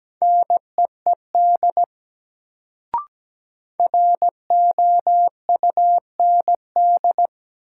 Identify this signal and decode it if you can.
Morse code, heard as a keyed tone, about 17 wpm, 700 Hz. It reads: NEED ROUND
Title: Sentences from Top 200 Words - New Word Drill - 17wpm